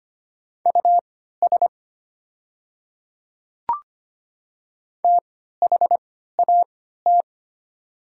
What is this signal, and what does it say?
Morse code, heard as a keyed tone, about 25 wpm, spaced as for 14 wpm, 700 Hz. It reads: US THAT
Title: 3x Character Spacing - Top 200 Words - 25wpm